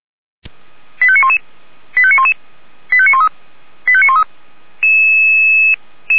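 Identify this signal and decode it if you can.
Five-tone selective call (ZVEI), heard as quick runs of five tones, tones of about 70 ms. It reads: 85911 85911 85913 85913
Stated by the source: ZVEI five-tone selective call received by SDR.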